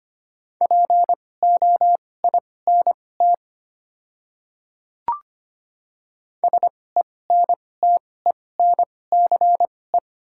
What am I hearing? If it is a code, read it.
Morse code, heard as a keyed tone, about 25 wpm, spaced as for 18 wpm, 700 Hz. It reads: POINT SENTENCE
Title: 2x Character Spacing - Top 200 Words - 25wpm